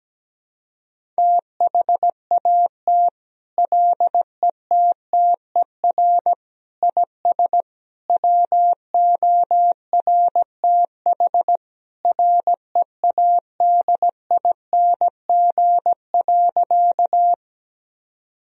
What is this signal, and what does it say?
Morse code, heard as a keyed tone, about 17 wpm, 700 Hz. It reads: THAT LETTER IS WORTH READING.
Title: Sentences from Top 1000 Words - Review Part 3 - 17wpm